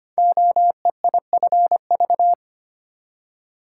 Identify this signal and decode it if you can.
Morse code, heard as a keyed tone, about 25 wpm, 700 Hz. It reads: OEIFV
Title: Code Groups - 5 Letters - 25wpm